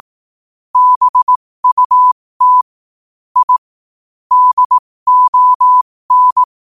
Morse code, heard as a keyed tone, about 18 wpm, spaced as for 15 wpm, 1000 Hz. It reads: BUT I DON
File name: qotdfiles / 15wpm